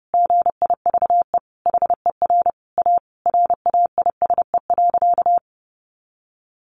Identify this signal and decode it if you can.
Morse code, heard as a keyed tone, about 30 wpm, 700 Hz. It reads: GIVE HER A RAISE.
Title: Easy Phrases 30wpm